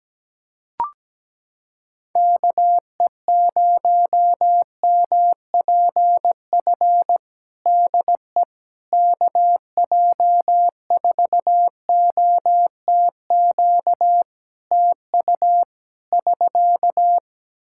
Morse code, heard as a keyed tone, about 17 wpm, 700 Hz. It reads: KE0MPF DE KJ4OTQ TU <SK>
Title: Practice QSO - Exchange 4 - Ending - 17wpm